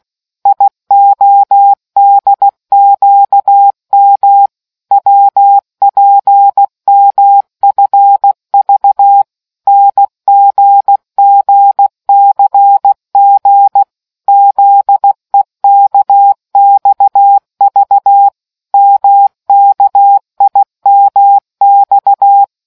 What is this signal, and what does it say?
Morse code, heard as a keyed tone, about 16 wpm, 800 Hz. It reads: IODQM WPMFV NGGCG ZEKXV MKIMX